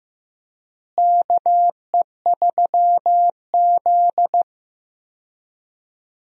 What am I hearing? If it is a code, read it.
Morse code, heard as a keyed tone, about 16 wpm, 700 Hz. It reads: KE3Z
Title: USA Call Signs - 4 Characters - Repeated 3x - 15wpm